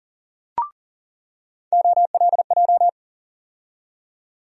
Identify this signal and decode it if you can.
Morse code, heard as a keyed tone, about 40 wpm, 700 Hz. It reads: OLJ
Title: USA Call Sign Postfixes - 3 Letters 40wpm